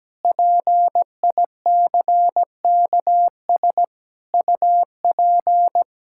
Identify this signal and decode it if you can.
Morse code, heard as a keyed tone, about 17 wpm, 700 Hz. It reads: PICKS UP
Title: Pangram Challenge v2 - 17wpm